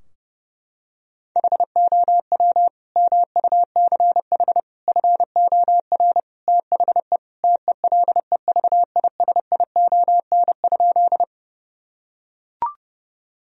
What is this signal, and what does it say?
Morse code, heard as a keyed tone, about 30 wpm, 700 Hz. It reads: HOW MUCH FOR THE TELEVISION?
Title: Sentences from Top 1000 Words - Review Part 4 - 30wpm